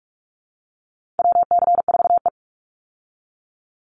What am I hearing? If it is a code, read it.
Morse code, heard as a keyed tone, about 44 wpm, 700 Hz. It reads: WC4E
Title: Call Signs - 45wpm